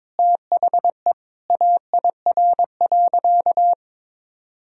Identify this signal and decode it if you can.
Morse code, heard as a keyed tone, about 22 wpm, 700 Hz. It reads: THE AIR.
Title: Sentences from Top 1000 Words - Review Part 1 - 22wpm